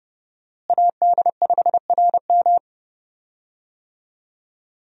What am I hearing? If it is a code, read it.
Morse code, heard as a keyed tone, about 30 wpm, 700 Hz. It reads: AD5RM
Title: USA Call Signs - 5 Characters - Repeated 2x - 30wpm